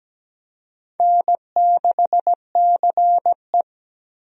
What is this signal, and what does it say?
Morse code, heard as a keyed tone, about 17 wpm, 700 Hz. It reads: N6CE
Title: USA Call Signs - 4 Characters - Repeated 3x - 17wpm